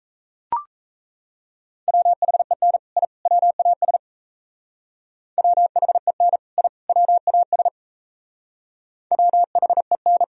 Morse code, heard as a keyed tone, about 34 wpm, 700 Hz. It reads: WHEN I WAS WHEN I WAS WHEN
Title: Speed-Racing - Sets of 3 Words 28wpm